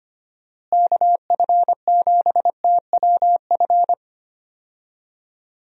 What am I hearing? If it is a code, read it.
Morse code, heard as a keyed tone, about 25 wpm, 700 Hz. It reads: KF7TWF
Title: USA Call Signs - 6 Characters - Repeated 3x - 25wpm